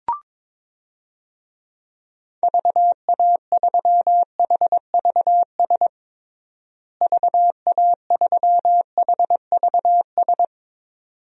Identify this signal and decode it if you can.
Morse code, heard as a keyed tone, about 22 wpm, 700 Hz. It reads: VA3HVS VA3HVS
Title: DXCC-100 Call Signs - 6 Characters - Repeated 2x - 22wpm